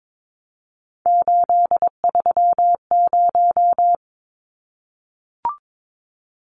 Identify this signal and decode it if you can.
Morse code, heard as a keyed tone, about 22 wpm, 700 Hz. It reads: 830
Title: Numbers - 3-Digit 22wpm